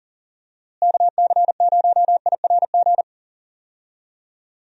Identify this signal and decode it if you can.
Morse code, heard as a keyed tone, about 40 wpm, 700 Hz. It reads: KC0IRG
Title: USA Call Signs - 6 Characters - Repeated 2x - 40wpm